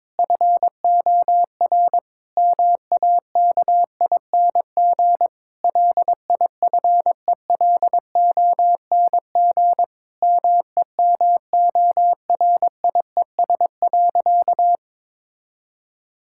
Morse code, heard as a keyed tone, about 21 wpm, 700 Hz. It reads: FOR MAKING LIFELONG MEMORIES.
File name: Difficult Sentences with Hints - 22wpm